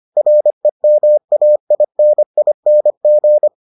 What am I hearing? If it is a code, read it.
Morse code, heard as a keyed tone, about 25 wpm, 600 Hz. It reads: REMAINING